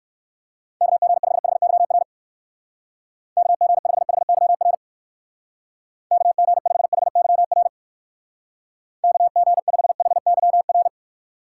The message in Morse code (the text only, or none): KG5HYR KG5HYR KG5HYR KG5HYR